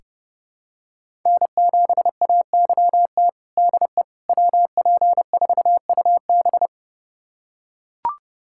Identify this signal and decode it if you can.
Morse code, heard as a keyed tone, about 30 wpm, 700 Hz. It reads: N7AYT DE WP4UB
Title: USA Callsign DE Callsign - 5 Characters - 30wpm